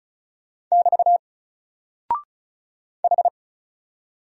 Morse code, heard as a keyed tone, about 34 wpm, 700 Hz. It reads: <BT> H
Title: Single Character (Letter:Number:Punctuation:Prosign) 35wpm